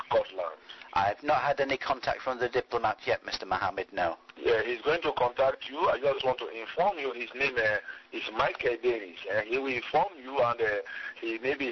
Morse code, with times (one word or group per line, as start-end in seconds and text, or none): none